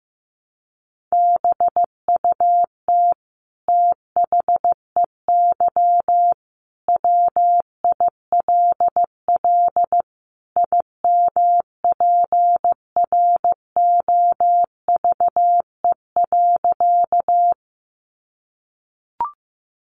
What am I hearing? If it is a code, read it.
Morse code, heard as a keyed tone, about 15 wpm, 700 Hz. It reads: BUT THEY WILL IMPROVE.
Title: Sentences from Top 800 Words 15wpm